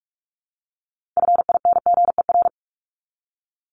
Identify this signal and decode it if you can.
Morse code, heard as a keyed tone, about 44 wpm, 700 Hz. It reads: FINGER
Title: Top 1501-1750 Words 45wpm